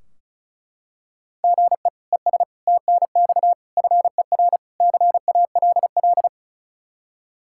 Morse code, heard as a keyed tone, about 36 wpm, 700 Hz. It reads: GE ES TNX FER CALL
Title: K1USN SST QSO Practice 35wpm